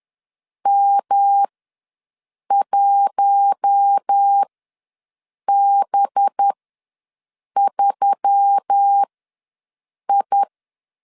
Morse code, heard as a keyed tone, about 10 wpm, spaced as for 6 wpm, 800 Hz. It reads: M1B3I
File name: LEZIONI / 6 PKX0 / 6 PKX ws 10wpm 3706